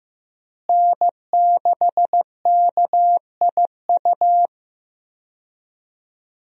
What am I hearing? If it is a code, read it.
Morse code, heard as a keyed tone, about 15 wpm, 700 Hz. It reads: N6KIU
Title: USA Call Signs - 5 Characters - Repeated 3x - 15wpm